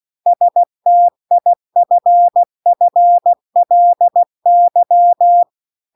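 Morse code, heard as a keyed tone, about 16 wpm, 700 Hz. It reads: STIFFLY
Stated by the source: Morse practice text at 16 WPM.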